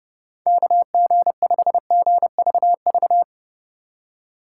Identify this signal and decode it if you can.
Morse code, heard as a keyed tone, about 32 wpm, 700 Hz. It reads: KG5GVV